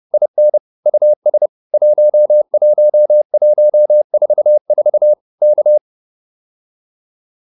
Morse code, heard as a keyed tone, about 30 wpm, 600 Hz. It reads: IN US 11144 K